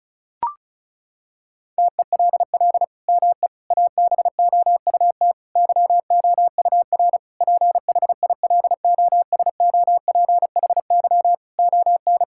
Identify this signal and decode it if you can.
Morse code, heard as a keyed tone, about 34 wpm, 700 Hz. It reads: TELL ME ABOUT YOUR PHILOSOPHY ON
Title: Sentences from Top 1000 Words - Review Part 6 - 35wpm